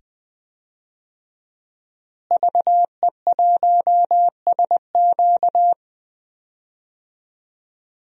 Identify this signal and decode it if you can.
Morse code, heard as a keyed tone, about 20 wpm, 700 Hz. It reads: VE1SQ
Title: DXCC-100 Call Signs - 5 Characters - Repeated 2x - 20wpm